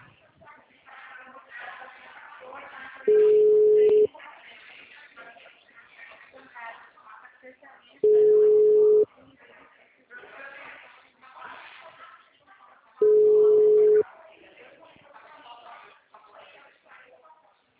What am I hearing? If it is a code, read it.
Ringback tone (Europe, 425 Hz), with speech.
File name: out- -2012- - - .699